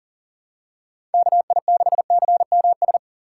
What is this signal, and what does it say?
Morse code, heard as a keyed tone, about 40 wpm, 700 Hz. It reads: KI6CMS
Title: USA Call Signs - 6 Characters - 2x Character Spacing - 40wpm